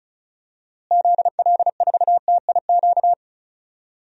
Morse code, heard as a keyed tone, about 34 wpm, 700 Hz. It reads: ZL4TIQ
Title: DXCC-100 Call Signs - 6 Characters - Repeated 3x - 35wpm